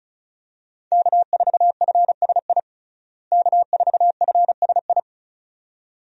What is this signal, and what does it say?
Morse code, heard as a keyed tone, about 36 wpm, 700 Hz. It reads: K4FSI K4FSI
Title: USA Call Signs - 5 Characters - Repeated 2x - 35wpm